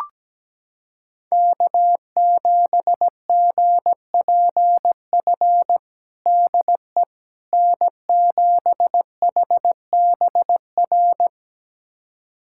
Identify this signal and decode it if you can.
Morse code, heard as a keyed tone, about 17 wpm, 700 Hz. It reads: K7GPF DE N7HBR
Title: USA Callsign DE Callsign - 5 Characters - 17wpm